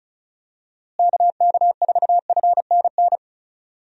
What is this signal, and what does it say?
Morse code, heard as a keyed tone, about 34 wpm, 700 Hz. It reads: KK4FNN
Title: USA Call Signs - 6 Characters - Repeated 2x - 35wpm